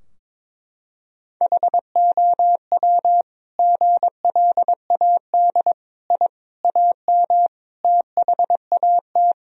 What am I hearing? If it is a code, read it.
Morse code, heard as a keyed tone, about 22 wpm, 700 Hz. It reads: HOW GLAD I AM THAT
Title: Sentences from Top 1000 Words - Review Part 2 - 22wpm